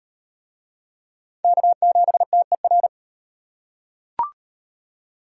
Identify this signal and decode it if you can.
Morse code, heard as a keyed tone, about 34 wpm, 700 Hz. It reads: K7TER